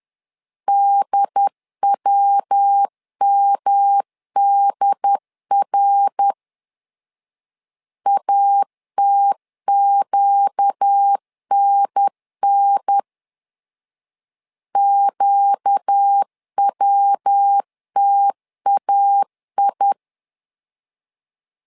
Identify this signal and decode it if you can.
Morse code, heard as a keyed tone, about 10 wpm, 800 Hz. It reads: DWMDR ATQNN QWTAI
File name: LEZIONI / 4 Q 10wpm 2210